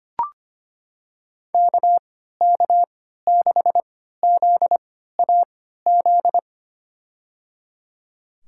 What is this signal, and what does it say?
Morse code, heard as a keyed tone, about 25 wpm, spaced as for 14 wpm, 700 Hz. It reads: KK6ZAZ
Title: USA Call Signs - 3x Character Spacing - 25wpm